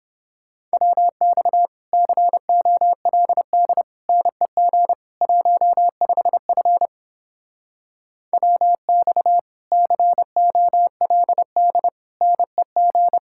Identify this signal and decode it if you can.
Morse code, heard as a keyed tone, about 31 wpm, 700 Hz. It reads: WX COLD NEG 15F WX COLD NEG